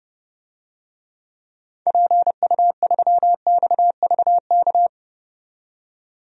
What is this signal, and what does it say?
Morse code, heard as a keyed tone, about 30 wpm, 700 Hz. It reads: PU3XVK
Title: DXCC-100 Call Signs - 6 Characters - 30wpm